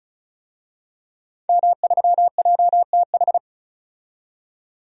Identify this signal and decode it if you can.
Morse code, heard as a keyed tone, about 36 wpm, 700 Hz. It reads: M3JTH
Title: DXCC-100 Call Signs - 5 Characters - 35wpm